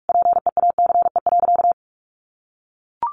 Morse code, heard as a keyed tone, about 44 wpm, 700 Hz. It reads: PEACE.
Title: Sentences from Top 1500 Words 45wpm